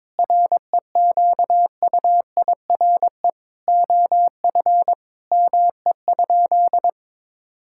Morse code, heard as a keyed tone, about 21 wpm, 700 Hz. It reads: REQUIRE OF ME?
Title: Sentences from Top 1000 Words - Review Part 6 - 22wpm